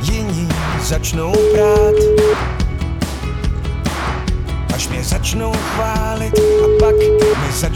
Ringback tone (Europe, 425 Hz), with noise.